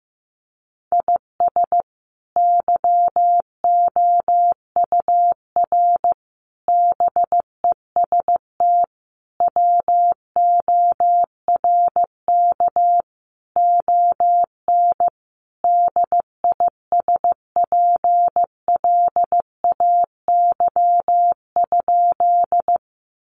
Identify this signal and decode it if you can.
Morse code, heard as a keyed tone, about 15 wpm, 700 Hz. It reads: IS YOUR BEST WORK ON DISPLAY?
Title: Sentences from Top 1200 Words 15wpm